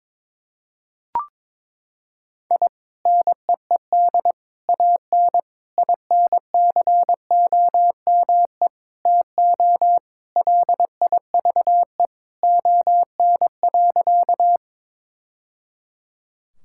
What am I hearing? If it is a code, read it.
Morse code, heard as a keyed tone, about 22 wpm, 700 Hz. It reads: I NEED AN INCOME TO LIVE ON.
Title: Sentences from Top 1000 Words - Review Part 5 - 22wpm